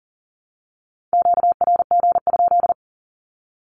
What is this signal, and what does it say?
Morse code, heard as a keyed tone, about 40 wpm, 700 Hz. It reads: QRG?
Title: Q-Codes 40wpm